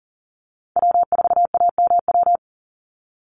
Morse code, heard as a keyed tone, about 40 wpm, 700 Hz. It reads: W4AMW
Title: NA SOTA Chaser Callsigns 40wpm